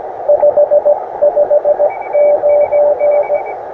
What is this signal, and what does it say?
Morse code, heard as a keyed tone, about 17 wpm, 590 Hz, with radio noise. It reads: 559
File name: attachments / us7ign a 20m